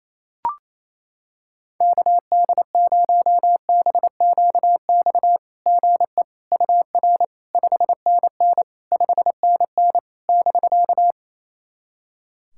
Morse code, heard as a keyed tone, about 28 wpm, 690 Hz. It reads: KD0BQX GE UR 5NN 5NN <BK>
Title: SOTA QSO Practice 28wpm